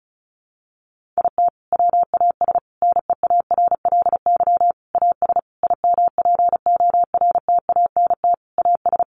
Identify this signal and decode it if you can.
Morse code, heard as a keyed tone, about 34 wpm, 700 Hz. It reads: IT WAS NEARLY AS IMPORTANT AS
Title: Sentences from Top 1400 Words 35wpm